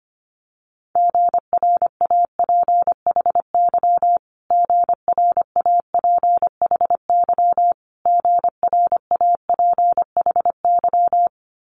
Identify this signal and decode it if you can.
Morse code, heard as a keyed tone, about 25 wpm, 700 Hz. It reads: GRAPHY GRAPHY GRAPHY
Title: Multi-letter Phonemes Repeated 3x 25wpm